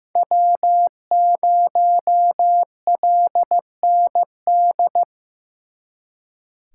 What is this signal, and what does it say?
Morse code, heard as a keyed tone, about 18 wpm, 700 Hz. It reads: W0LND
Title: Speed Racing - USA 1x3 Call Signs 15wpm